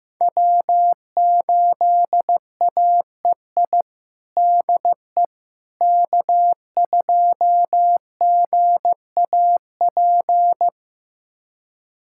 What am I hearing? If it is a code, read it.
Morse code, heard as a keyed tone, about 16 wpm, 700 Hz. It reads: W8AEI DE K2GAP
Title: USA Callsign DE Callsign - 5 Characters - 15wpm